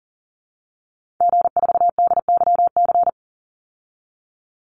Morse code, heard as a keyed tone, about 40 wpm, 700 Hz. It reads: G4DYC
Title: Call Signs Repeated 3x - 40wpm